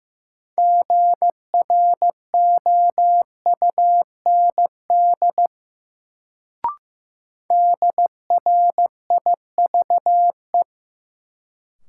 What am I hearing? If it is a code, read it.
Morse code, heard as a keyed tone, about 15 wpm, 700 Hz. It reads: GROUND DRIVE
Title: Sentences from Top 500 Words - New Word Drill - 15wpm